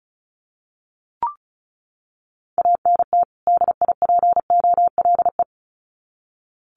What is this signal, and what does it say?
Morse code, heard as a keyed tone, about 34 wpm, 700 Hz. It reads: ANT DIPOLE
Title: Common QSO Phrases 35wpm